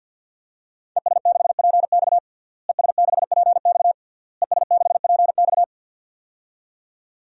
Morse code, heard as a keyed tone, about 53 wpm, 700 Hz. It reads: ES6PX ES6PX ES6PX
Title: DXCC-100 Call Signs - 5 Characters - Repeated 3x - 50wpm